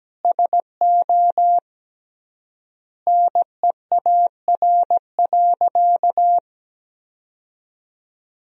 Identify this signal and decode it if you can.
Morse code, heard as a keyed tone, about 17 wpm, 700 Hz. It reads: SO NEAR.
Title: 3x Word-Spacing - Sentences from Top 300 Words - 17wpm